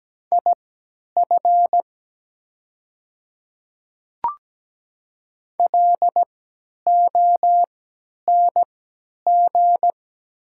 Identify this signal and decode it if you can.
Morse code, heard as a keyed tone, about 17 wpm, spaced as for 10 wpm, 690 Hz. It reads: IF LONG
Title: 3x Character Spacing - Top 100 Words - 17wpm